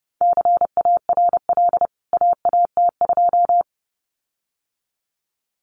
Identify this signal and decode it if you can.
Morse code, heard as a keyed tone, about 30 wpm, 700 Hz. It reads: CARL AAT2
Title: CWT Exchange 30wpm